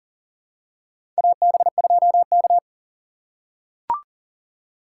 Morse code, heard as a keyed tone, about 40 wpm, 700 Hz. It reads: AB2K
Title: USA Call Signs - 4 Characters - 3x Character Spacing - 40wpm